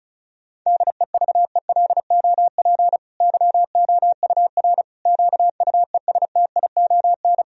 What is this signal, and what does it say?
Morse code, heard as a keyed tone, about 36 wpm, 700 Hz. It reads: DEVELOP YOUR QUESTION
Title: Sentences from Top 1000 Words - Review Part 2 - 35wpm